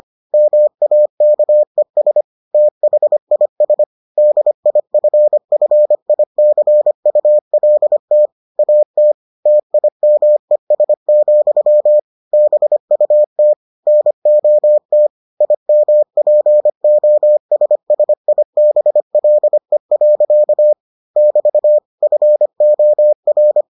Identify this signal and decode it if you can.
Morse code, heard as a keyed tone, about 25 wpm, 600 Hz. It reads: MAKES THIS DIFFICULT AT TIMES, BUT NOT IMPOSSIBLE. <BT> FOR